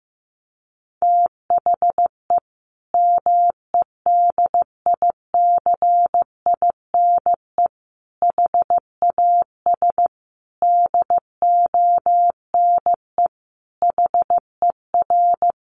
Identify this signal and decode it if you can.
Morse code, heard as a keyed tone, about 15 wpm, 700 Hz. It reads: THE MEDICINE HAS DONE HER
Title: Sentences from Top 900 Words 15wpm